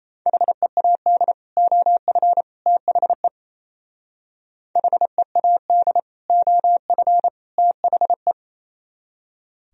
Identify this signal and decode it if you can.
Morse code, heard as a keyed tone, about 37 wpm, 700 Hz. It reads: HEAD OF THE HEAD OF THE